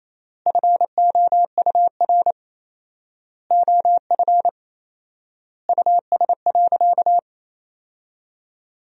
Morse code, heard as a keyed tone, about 28 wpm, 700 Hz. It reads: FOUR OF US.
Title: 4x Word-Spacing - Sentences from Top 300 Words - 28wpm